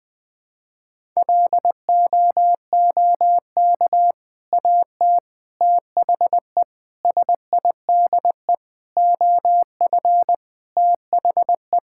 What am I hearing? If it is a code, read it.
Morse code, heard as a keyed tone, about 21 wpm, 700 Hz. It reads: LOOK AT THE SIDE OF THE
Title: Sentences with 3-Word Phrases - Spoken Beginning - 20wpm